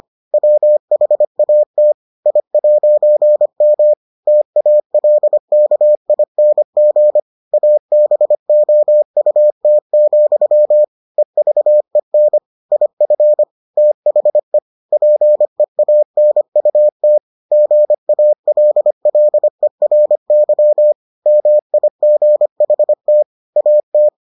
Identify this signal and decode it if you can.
Morse code, heard as a keyed tone, about 25 wpm, 600 Hz. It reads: WHAT I'M TALKING ABOUT, EVEN IF THE PEANUT GALLERY MIGHT AT